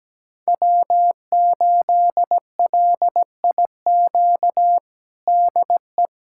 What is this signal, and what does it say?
Morse code, heard as a keyed tone, about 17 wpm, 700 Hz. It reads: W8LIQ DE
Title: USA Callsign DE Callsign 17wpm